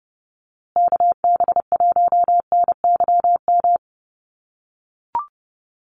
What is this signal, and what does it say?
Morse code, heard as a keyed tone, about 30 wpm, 700 Hz. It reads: KB1NYM
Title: USA Call Signs Repeated 2x 30wpm